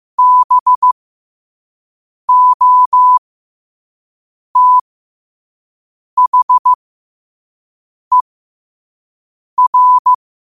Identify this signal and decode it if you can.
Morse code, heard as a keyed tone, about 15 wpm, spaced as for 5 wpm, 1000 Hz. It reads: BOTHER
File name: qotdfiles / 5wpm